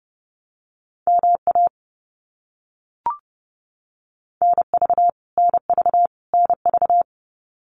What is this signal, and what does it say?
Morse code, heard as a keyed tone, about 30 wpm, 700 Hz. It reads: MA NV NV NV
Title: US States Repeated 3x 30wpm